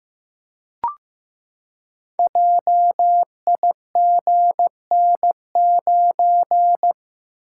Morse code, heard as a keyed tone, about 15 wpm, 700 Hz. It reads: JIGN9
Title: Code Groups - 5 Letter:Numbers - 15wpm